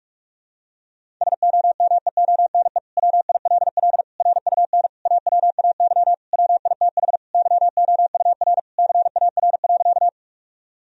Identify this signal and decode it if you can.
Morse code, heard as a keyed tone, about 46 wpm, 700 Hz. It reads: SOMEONE WILL RUN AWAY WITH YOUR CAR.
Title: Sentences from Top 1000 Words - Review Part 1 - 45wpm